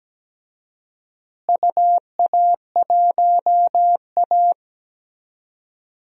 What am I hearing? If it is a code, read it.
Morse code, heard as a keyed tone, about 17 wpm, 700 Hz. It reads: UA1A
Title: DXCC-100 Call Signs - 4 Characters - Repeated 3x - 17wpm